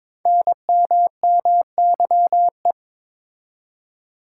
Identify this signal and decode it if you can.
Morse code, heard as a keyed tone, about 22 wpm, 700 Hz. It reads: NMMYE
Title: Code Groups - 5 Letters - 22wpm